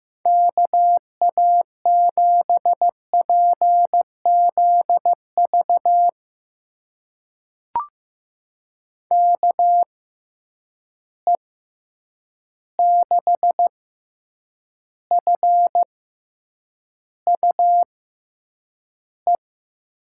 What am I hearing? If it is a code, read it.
Morse code, heard as a keyed tone, about 15 wpm, spaced as for 8 wpm, 700 Hz. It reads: KA7PZV KE6FUE
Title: USA Call Signs - 6 Characters - 6x Character Spacing - 15wpm